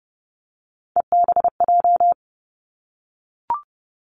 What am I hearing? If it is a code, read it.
Morse code, heard as a keyed tone, about 30 wpm, 700 Hz. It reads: EBJ
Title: Three Challenging Letter-Number Combinations 30wpm